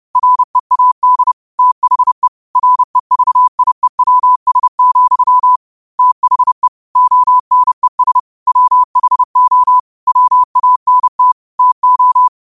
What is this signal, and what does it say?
Morse code, heard as a keyed tone, about 30 wpm, 1000 Hz. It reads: READ THE REVIEWS, THE ONES WHO WANT TO